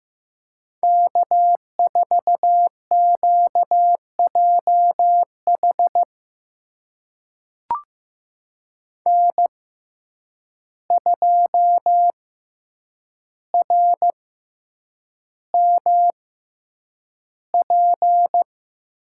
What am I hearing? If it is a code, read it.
Morse code, heard as a keyed tone, about 15 wpm, spaced as for 8 wpm, 700 Hz. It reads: K4QJH N2RMP
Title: USA Call Signs - 5 Characters - 6x Character Spacing - 15wpm